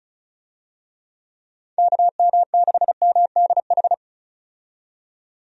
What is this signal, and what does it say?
Morse code, heard as a keyed tone, about 34 wpm, 700 Hz. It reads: KM6MDH
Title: USA Call Signs - 6 Characters - 35wpm